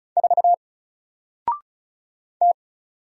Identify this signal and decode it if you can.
Morse code, heard as a keyed tone, about 36 wpm, 700 Hz. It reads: 4 T
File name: Single-Letter-Number-35wpm